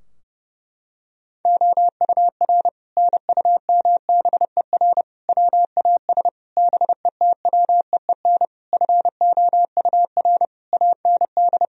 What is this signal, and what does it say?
Morse code, heard as a keyed tone, about 32 wpm, 700 Hz. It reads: OUR NUMBER WAS BETWEEN FOUR AND
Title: Sentences from Top 1000 Words - Review Part 2 - 30wpm